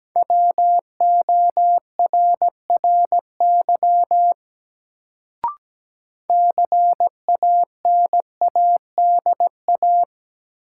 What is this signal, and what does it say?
Morse code, heard as a keyed tone, about 17 wpm, 700 Hz. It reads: WORRY CANADA
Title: Sets of 5 Words - New Word Drill - 17wpm